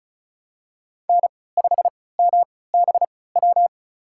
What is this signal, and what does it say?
Morse code, heard as a keyed tone, about 34 wpm, spaced as for 20 wpm, 700 Hz. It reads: N5MBW
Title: USA Call Signs - 5 Characters - 3x Character Spacing - 35wpm